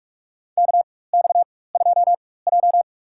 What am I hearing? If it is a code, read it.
Morse code, heard as a keyed tone, about 46 wpm, spaced as for 21 wpm, 700 Hz. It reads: KX2J
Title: USA Call Signs - 4 Characters - 4x Character Spacing - 45wpm